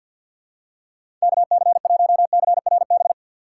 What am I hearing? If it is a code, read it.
Morse code, heard as a keyed tone, about 53 wpm, 690 Hz. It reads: KC1CRB
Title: USA Call Signs - 2x Character Spacing - 50wpm